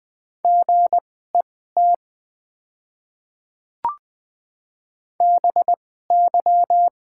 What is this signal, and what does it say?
Morse code, heard as a keyed tone, about 21 wpm, spaced as for 15 wpm, 700 Hz. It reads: GET BY
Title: 2x Character Spacing - Top 500 Words - 20wpm